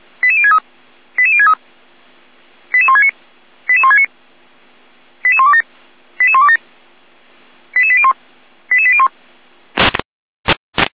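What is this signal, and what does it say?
Five-tone selective call (ZVEI), heard as quick runs of five tones, tones of about 70 ms. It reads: 88063 88063 88169 88169 88127 88127 88801 88801
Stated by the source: ZVEI five-tone call received via SDR.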